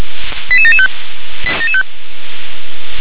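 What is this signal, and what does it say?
Five-tone selective call (ZVEI), heard as quick runs of five tones, tones of about 70 ms. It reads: 88775
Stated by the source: ZVEI five-tone selective call received with an SDR.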